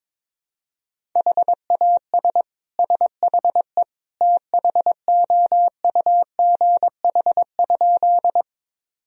Morse code, heard as a keyed tone, about 21 wpm, 700 Hz. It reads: HAS SHE THOUGH?
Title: Sentences from Top 1000 Words - Review Part 2 - 22wpm